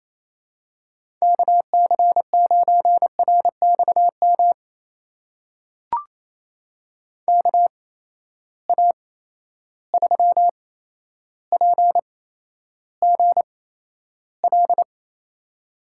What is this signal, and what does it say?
Morse code, heard as a keyed tone, about 28 wpm, spaced as for 12 wpm, 700 Hz. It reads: KC9RXM KA3PGL
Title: USA Call Signs - 6 Characters - 8x Character Spacing - 28wpm